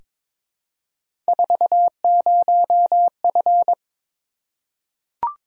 Morse code, heard as a keyed tone, about 22 wpm, 700 Hz. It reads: 40F
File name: practice / 22wpm / Three Letter-Number Combinations - 22wpm